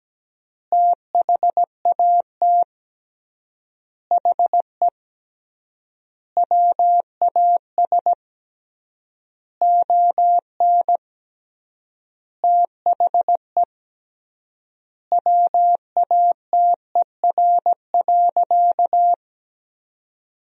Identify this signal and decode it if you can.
Morse code, heard as a keyed tone, about 17 wpm, 700 Hz. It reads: THAT HE WAS ON THE WATER.
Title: 3x Word-Spacing - Sentences from Top 500 Words - 17wpm